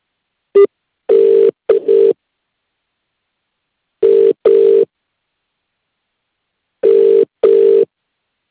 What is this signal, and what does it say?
Ringback tone (UK), 400 + 450 Hz.